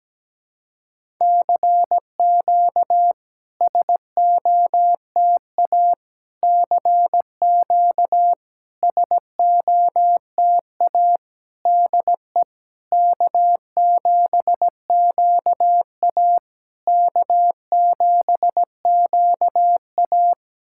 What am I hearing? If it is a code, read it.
Morse code, heard as a keyed tone, about 17 wpm, 690 Hz. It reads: CQ SOTA CQ SOTA DE K7QA K7QA
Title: SOTA QSO Practice 17wpm